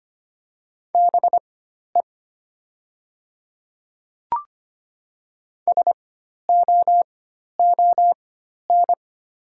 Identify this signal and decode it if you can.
Morse code, heard as a keyed tone, about 25 wpm, spaced as for 12 wpm, 690 Hz. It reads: BE SOON
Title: 4x Character Spacing - Top 500 Words - 25wpm